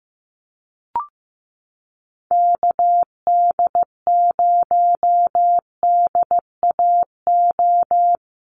Morse code, heard as a keyed tone, about 15 wpm, 700 Hz. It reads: KD0DAO
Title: USA Call Signs 15wpm